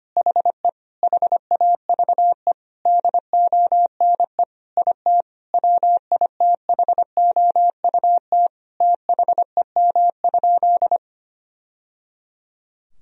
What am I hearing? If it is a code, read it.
Morse code, heard as a keyed tone, about 25 wpm, 700 Hz. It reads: HE HAVE DONE IT WITHOUT THEM?
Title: Sentences from Top 1000 Words - Review Part 3 - 25wpm